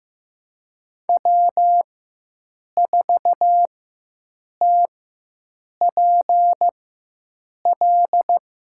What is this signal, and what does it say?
Morse code, heard as a keyed tone, about 15 wpm, spaced as for 7 wpm, 700 Hz. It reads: W4TPL
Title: USA Call Signs - 5 Characters - 4x Character Spacing - 15wpm